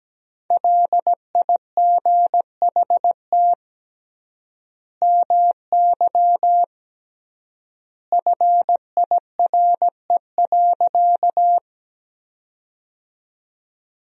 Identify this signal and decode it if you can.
Morse code, heard as a keyed tone, about 17 wpm, 700 Hz. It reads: LIGHT MY FIRE.
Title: 3x Word-Spacing - Sentences from Top 400 Words - 17wpm